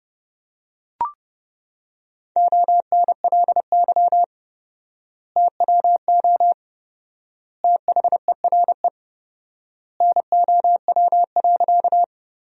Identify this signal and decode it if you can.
Morse code, heard as a keyed tone, about 30 wpm, 700 Hz. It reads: ONLY TWO THERE NOW.
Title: 4x Word-Spacing - Sentences from Top 200 Words - 30wpm